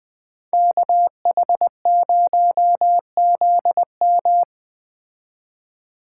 Morse code, heard as a keyed tone, about 21 wpm, 700 Hz. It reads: KH0ZM